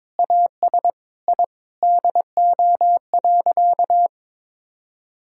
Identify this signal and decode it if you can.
Morse code, heard as a keyed tone, about 21 wpm, 700 Hz. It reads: AS I DO.
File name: practice / 22wpm / Sentences with 8-Word Phrases - 22wpm